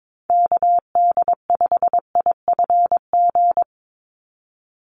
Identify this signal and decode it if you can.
Morse code, heard as a keyed tone, about 22 wpm, 700 Hz. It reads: KD5IFG
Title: USA Call Signs 22wpm